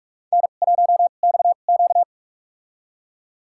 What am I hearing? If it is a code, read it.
Morse code, heard as a keyed tone, about 44 wpm, spaced as for 32 wpm, 700 Hz. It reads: N1XQ
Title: USA Call Signs - 4 Characters - 2x Character Spacing - 45wpm